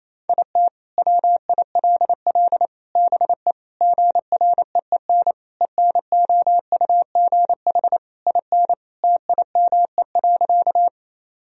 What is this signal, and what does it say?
Morse code, heard as a keyed tone, about 28 wpm, 700 Hz. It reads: IT WILL BE GREEN ENOUGH IN TIME.
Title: Sentences from Top 1000 Words - Review Part 2 - 28wpm